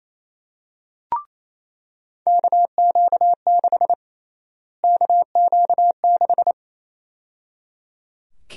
Morse code, heard as a keyed tone, about 28 wpm, 700 Hz. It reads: KQ6 KQ6
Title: USA Call Sign Prefixes - Repeated 2x - 28wpm